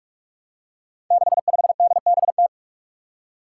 Morse code, heard as a keyed tone, about 46 wpm, 700 Hz. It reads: 65DBT
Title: Five Challenging Letter-Number Combinations 45wpm